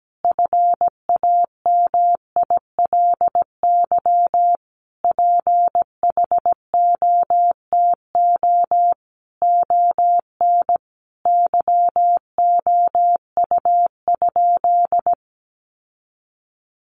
Morse code, heard as a keyed tone, about 17 wpm, 700 Hz. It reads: FAMILY PHOTO ON YOU?
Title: Sentences from Top 1200 Words 17wpm